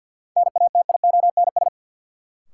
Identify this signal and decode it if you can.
Morse code, heard as a keyed tone, about 49 wpm, 700 Hz. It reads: NATIONS
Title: Sets of 3 Words - New Word Drill - 50wpm